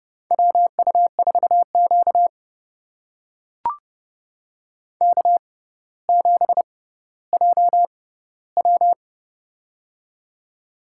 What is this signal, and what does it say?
Morse code, heard as a keyed tone, about 30 wpm, spaced as for 15 wpm, 700 Hz. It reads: WU4Q K7JW
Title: USA Call Signs - 4 Characters - 6x Character Spacing - 30wpm